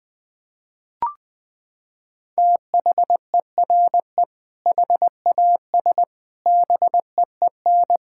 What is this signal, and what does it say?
Morse code, heard as a keyed tone, about 20 wpm, 700 Hz. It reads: THERE HAS BEEN
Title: Sentences with 3-Word Phrases - 20wpm